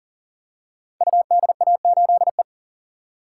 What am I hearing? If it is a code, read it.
Morse code, heard as a keyed tone, about 38 wpm, 700 Hz. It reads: UDA8E
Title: Code Groups - 5 Letter:Numbers - 40wpm